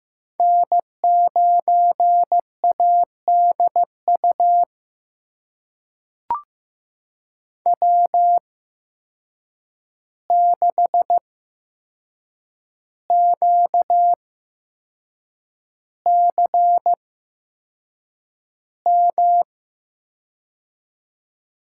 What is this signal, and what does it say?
Morse code, heard as a keyed tone, about 15 wpm, spaced as for 6 wpm, 700 Hz. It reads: N9ADU W6QCM
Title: USA Call Signs - 5 Characters - 8x Character Spacing - 15wpm